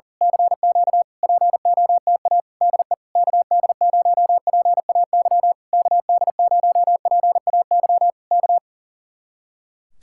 Morse code, heard as a keyed tone, about 38 wpm, 700 Hz. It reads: CQ POTA DE KD0PAY KD0PAY K
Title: Speed-Racing Common QSO Phrases Encore - 40wpm